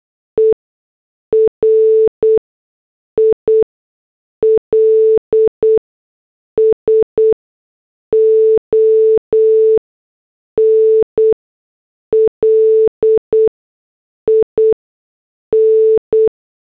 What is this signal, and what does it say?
Morse code, heard as a keyed tone, about 8 wpm, spaced as for 6 wpm, 440 Hz. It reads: ERILSON LIN